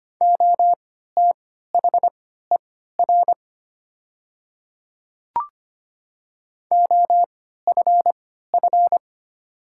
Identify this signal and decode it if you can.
Morse code, heard as a keyed tone, about 25 wpm, spaced as for 14 wpm, 700 Hz. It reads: OTHER OFF
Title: 3x Character Spacing - Top 200 Words - 25wpm